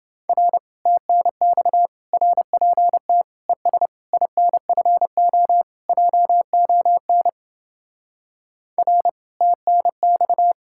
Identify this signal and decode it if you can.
Morse code, heard as a keyed tone, about 31 wpm, 700 Hz. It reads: R TNX RPT ES INFO JON R TNX